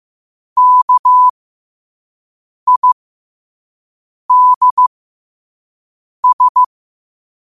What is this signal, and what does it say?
Morse code, heard as a keyed tone, about 15 wpm, spaced as for 5 wpm, 1000 Hz. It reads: KIDS
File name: qotdfiles / 5wpm